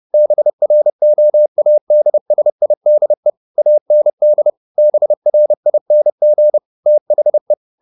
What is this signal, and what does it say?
Morse code, heard as a keyed tone, about 30 wpm, 600 Hz. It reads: BROADSIDE AND BRING THE